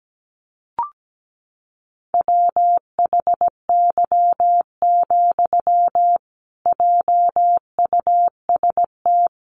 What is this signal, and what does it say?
Morse code, heard as a keyed tone, about 17 wpm, 700 Hz. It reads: WHY, JUST
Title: Sentences from Top 1000 Words 17wpm